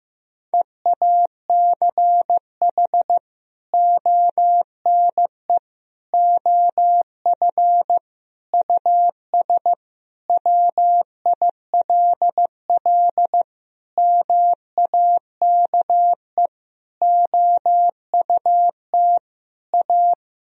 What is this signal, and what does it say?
Morse code, heard as a keyed tone, about 15 wpm, 700 Hz. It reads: EACH ONE OF US WILL MAKE OUT A
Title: Sentences from Top 1000 Words - Review Part 2 - 15wpm